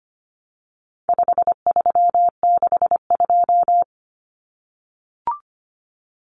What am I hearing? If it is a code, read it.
Morse code, heard as a keyed tone, about 25 wpm, 700 Hz. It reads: 5362
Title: Common Cut Numbers - 4-Digit 25wpm